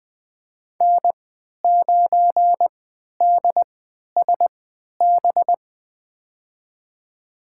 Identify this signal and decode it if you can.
Morse code, heard as a keyed tone, about 20 wpm, spaced as for 11 wpm, 700 Hz. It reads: N9DSB
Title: USA Call Signs - 5 Characters - 3x Character Spacing - 20wpm